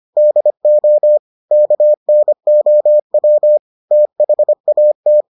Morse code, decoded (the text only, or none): DO KNOW THAT